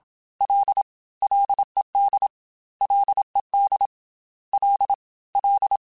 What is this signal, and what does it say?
Morse code, heard as a keyed tone, about 27 wpm, 800 Hz. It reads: L LED LED L L
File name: F and L / F and L mp3 / F and L Prefixes 3x2 Twice @25